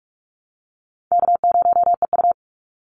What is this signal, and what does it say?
Morse code, heard as a keyed tone, about 44 wpm, 700 Hz. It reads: K0EU
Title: Call Signs Repeated 2x - 45wpm